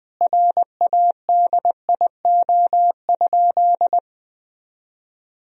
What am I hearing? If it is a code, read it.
Morse code, heard as a keyed tone, about 20 wpm, 700 Hz. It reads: RADIO?
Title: Sentences from Top 1000 Words - Review Part 3 - 20wpm